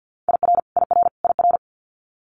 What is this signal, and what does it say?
Morse code, heard as a keyed tone, about 50 wpm, 700 Hz. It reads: IR IR IR
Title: Multi-letter Phonemes Repeated 3x 50wpm